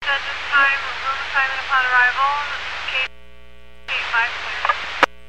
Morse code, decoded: none

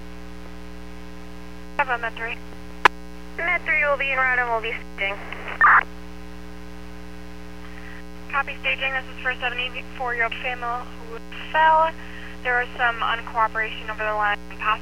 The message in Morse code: none

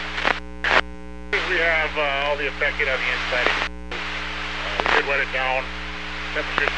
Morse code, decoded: none

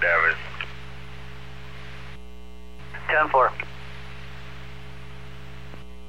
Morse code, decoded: none

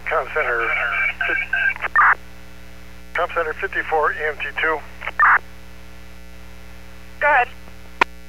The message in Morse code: none

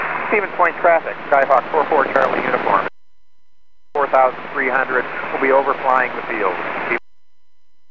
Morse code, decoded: none

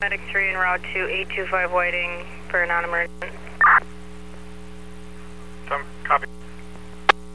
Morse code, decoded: none

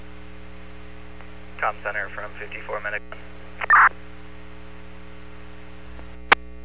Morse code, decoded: none

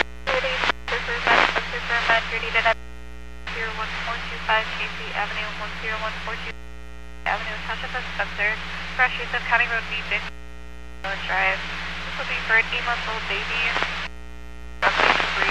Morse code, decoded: none

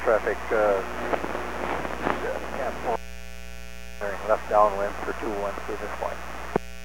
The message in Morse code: none

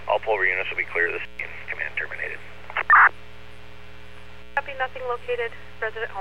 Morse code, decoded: none

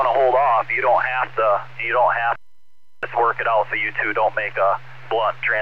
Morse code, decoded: none

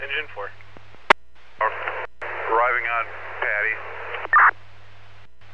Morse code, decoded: none

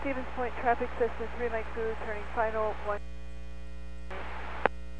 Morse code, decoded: none

